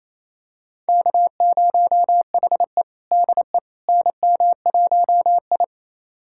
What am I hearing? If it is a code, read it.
Morse code, heard as a keyed tone, about 28 wpm, 700 Hz. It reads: K0HE DE NM1I
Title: USA Callsign DE Callsign - 4 Characters - 28wpm